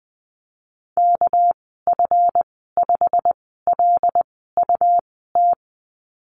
Morse code, heard as a keyed tone, about 19 wpm, spaced as for 14 wpm, 700 Hz. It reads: KF5LUT
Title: 207 Lesson 33g - Letter j - Call Signs 20wpm